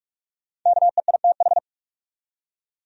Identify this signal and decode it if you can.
Morse code, heard as a keyed tone, about 48 wpm, 700 Hz. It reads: KEITH
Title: USA Common Names - Max 5 Letters - 45wpm